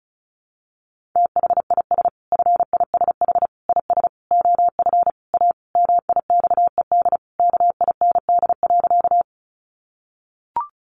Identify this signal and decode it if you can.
Morse code, heard as a keyed tone, about 34 wpm, 700 Hz. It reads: THIS FISH IS OF A MIXED KIND.
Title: Sentences from Top 1500 Words 35wpm